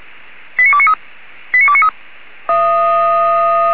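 Five-tone selective call (ZVEI), heard as quick runs of five tones, tones of about 70 ms. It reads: 89293 89293
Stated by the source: ZVEI five-tone selective call received by SDR.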